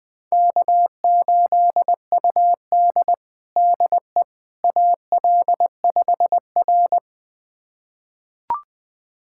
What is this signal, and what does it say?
Morse code, heard as a keyed tone, about 20 wpm, 700 Hz. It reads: K8UD DE AL5R
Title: USA Callsign DE Callsign - 4 Characters - 20wpm